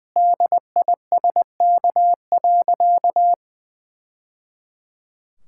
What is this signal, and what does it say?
Morse code, heard as a keyed tone, about 21 wpm, 700 Hz. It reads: DISK.
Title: Sentences from Top 1000 Words - Review Part 6 - 20wpm